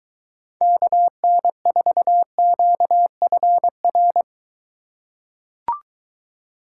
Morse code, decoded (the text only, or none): KN4QFR